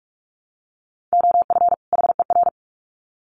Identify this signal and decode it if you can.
Morse code, heard as a keyed tone, about 44 wpm, 700 Hz. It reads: OF HER
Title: Sets of 2 Words 45wpm